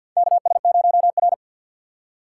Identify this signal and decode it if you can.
Morse code, heard as a keyed tone, about 52 wpm, 700 Hz. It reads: KS0R
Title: USA Call Signs - 4 Characters - 6x Character Spacing - 50wpm